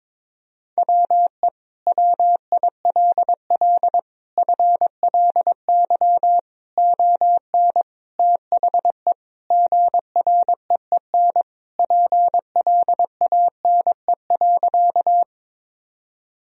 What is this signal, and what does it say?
Morse code, heard as a keyed tone, about 22 wpm, 700 Hz. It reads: WE WILL FLY ON THE GREEN PLANE.
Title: Sentences from Top 1000 Words - Review Part 3 - 22wpm